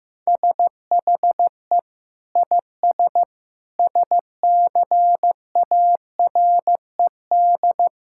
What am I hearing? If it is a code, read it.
Morse code, heard as a keyed tone, about 16 wpm, 700 Hz. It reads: SHE IS SCARED
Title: Sentences from Top 1000 Words - Review Part 6 - 15wpm